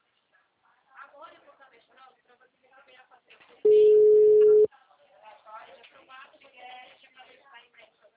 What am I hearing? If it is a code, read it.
Ringback tone (Europe, 425 Hz), with speech.